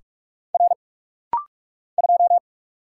Morse code, heard as a keyed tone, about 45 wpm, 700 Hz. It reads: R 2
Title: Single Letter-Number Alt - Rapid-Fire 45wpm